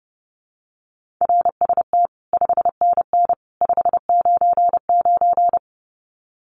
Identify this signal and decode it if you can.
Morse code, heard as a keyed tone, about 30 wpm, 700 Hz. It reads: RST 5NN 599
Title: Common QSO Phrases 30wpm